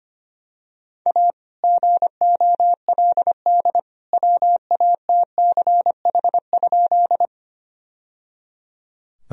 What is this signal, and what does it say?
Morse code, heard as a keyed tone, about 25 wpm, 700 Hz. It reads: A GOLD WATCH?
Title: Sentences from Top 1000 Words - Review Part 3 - 25wpm